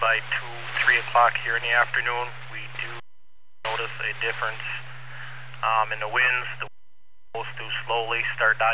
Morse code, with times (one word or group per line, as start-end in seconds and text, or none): none